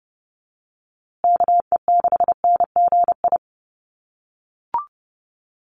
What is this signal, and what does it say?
Morse code, heard as a keyed tone, about 30 wpm, 700 Hz. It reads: KE6NGI
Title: USA Call Signs Repeated 2x 30wpm